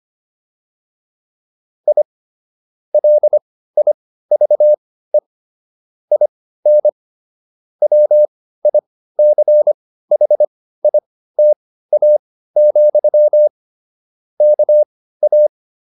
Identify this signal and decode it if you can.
Morse code, heard as a keyed tone, about 25 wpm, spaced as for 15 wpm, 600 Hz. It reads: I LIVE IN WICHITA, KA